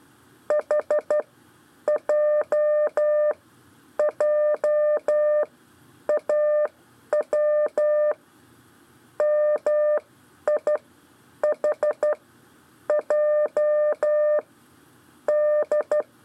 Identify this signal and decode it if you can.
Morse code, heard as a keyed tone, about 12 wpm, spaced as for 8 wpm, 600 Hz. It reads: HJJAW MIHJD
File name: 4 LEZIONE CW 1